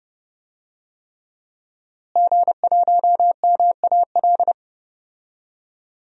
Morse code, heard as a keyed tone, about 30 wpm, 700 Hz. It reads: G1MAL
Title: DXCC-100 Call Signs - 5 Characters - Repeated 3x - 30wpm